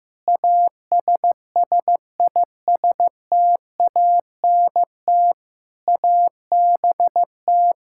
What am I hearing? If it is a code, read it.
Morse code, heard as a keyed tone, about 15 wpm, 700 Hz. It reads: ASSISTANT ABT
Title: K1USN SST QSO Practice 15wpm